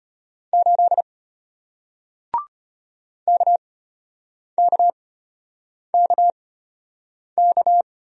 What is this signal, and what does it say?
Morse code, heard as a keyed tone, about 33 wpm, spaced as for 8 wpm, 700 Hz. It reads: 8 KKKK